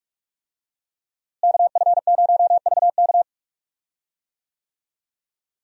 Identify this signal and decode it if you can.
Morse code, heard as a keyed tone, about 42 wpm, 700 Hz. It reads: KF0VK
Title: USA Call Signs - 5 Characters - Repeated 3x - 45wpm